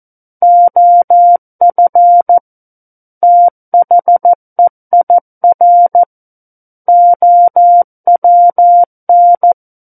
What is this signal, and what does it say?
Morse code, heard as a keyed tone, about 14 wpm, 700 Hz. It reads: OF THEIR OWN
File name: Book One / 05-Chapter 5 - 14 WPM